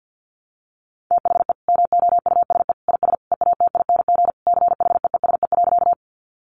Morse code, heard as a keyed tone, about 50 wpm, 700 Hz. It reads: THE MOUSE IS EATING CHEESE.
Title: Easy Phrases 50wpm